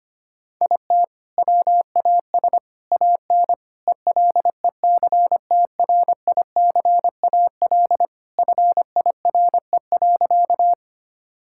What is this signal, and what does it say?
Morse code, heard as a keyed tone, about 25 wpm, 700 Hz. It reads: IT WAS AN ELECTRICAL FIRE.
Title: Sentences from Top 1000 Words - Review Part 6 - 25wpm